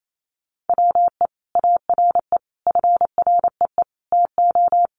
Morse code, heard as a keyed tone, about 28 wpm, 700 Hz. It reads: WE ARE FREE TO
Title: Sentences from Top 500 Words 28wpm